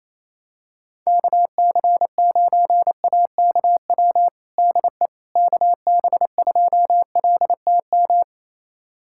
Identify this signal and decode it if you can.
Morse code, heard as a keyed tone, about 28 wpm, 700 Hz. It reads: KC9AKW DE KB2LTM
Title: USA Callsign DE Callsign 28wpm